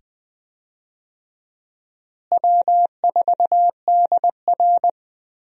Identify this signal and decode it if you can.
Morse code, heard as a keyed tone, about 20 wpm, 700 Hz. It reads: W4DR
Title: USA Call Signs - 4 Characters - Repeated 2x - 20wpm